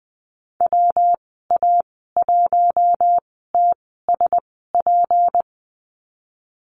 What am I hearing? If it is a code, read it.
Morse code, heard as a keyed tone, about 20 wpm, spaced as for 15 wpm, 700 Hz. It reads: WA1TSP
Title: 161 Lesson 26g - Letter g - Call Signs 20wpm